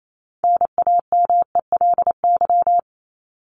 Morse code, heard as a keyed tone, about 28 wpm, 700 Hz. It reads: NAMELY
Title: LY Suffix - Top 100 Words with Base Words 28wpm